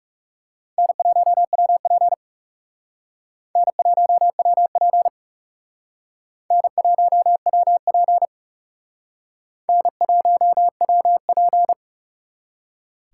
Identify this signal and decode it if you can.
Morse code, heard as a keyed tone, about 37 wpm, 700 Hz. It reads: N1WP N1WP N1WP N1WP
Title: Speed Racing - USA 1x2 Call Signs 30wpm